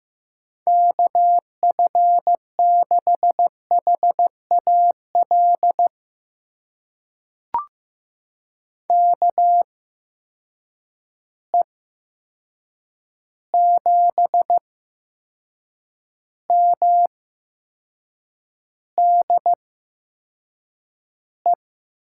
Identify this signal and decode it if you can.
Morse code, heard as a keyed tone, about 15 wpm, spaced as for 6 wpm, 700 Hz. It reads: KF6HAL KE7MDE
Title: USA Call Signs - 6 Characters - 8x Character Spacing - 15wpm